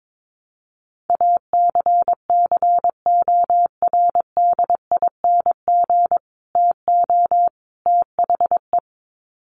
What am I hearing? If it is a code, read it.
Morse code, heard as a keyed tone, about 22 wpm, 700 Hz. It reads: ACCORDING TO THE
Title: Sets of 3 Words 22wpm